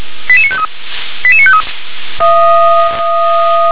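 Five-tone selective call (ZVEI), heard as quick runs of five tones, tones of about 70 ms. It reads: 88953 88953
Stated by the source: ZVEI five-tone selective call received by SDR.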